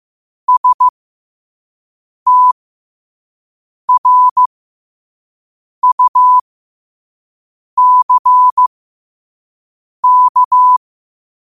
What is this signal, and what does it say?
Morse code, heard as a keyed tone, about 15 wpm, spaced as for 5 wpm, 1000 Hz. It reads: STRUCK
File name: 2024-10-07-5wpm